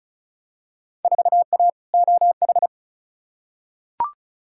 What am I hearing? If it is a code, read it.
Morse code, heard as a keyed tone, about 34 wpm, 700 Hz. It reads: 4A OH
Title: Field Day Exchanges - 35wpm